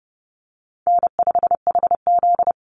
Morse code, heard as a keyed tone, about 30 wpm, 700 Hz. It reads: N5HZ
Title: Easy Call Signs - 30wpm